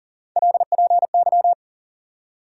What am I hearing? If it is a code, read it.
Morse code, heard as a keyed tone, about 40 wpm, 700 Hz. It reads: LPY
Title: USA Call Sign Postfixes - 3 Letters 40wpm